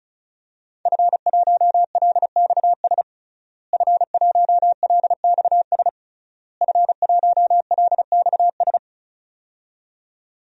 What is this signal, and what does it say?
Morse code, heard as a keyed tone, about 34 wpm, 700 Hz. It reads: F1LXS F1LXS F1LXS
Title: DXCC-100 Call Signs - 5 Characters - Repeated 3x - 35wpm